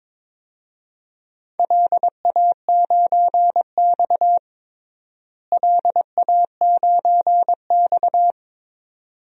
Morse code, heard as a keyed tone, about 22 wpm, 700 Hz. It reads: LA9X LA9X
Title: DXCC-100 Call Signs - 4 Characters - Repeated 2x - 22wpm